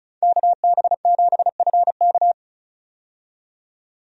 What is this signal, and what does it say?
Morse code, heard as a keyed tone, about 34 wpm, 700 Hz. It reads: KB7FK
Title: USA Call Signs - 5 Characters - 35wpm